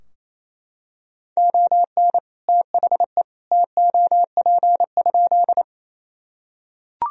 Morse code, heard as a keyed tone, about 28 wpm, 700 Hz. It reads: ON THE TOP?
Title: Sentences from Top 1000 Words - Review Part 2 - 28wpm